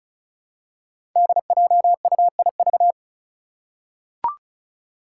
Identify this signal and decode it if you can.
Morse code, heard as a keyed tone, about 36 wpm, 700 Hz. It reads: DJUIV
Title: Five Challenging Letter-Number Combinations 35wpm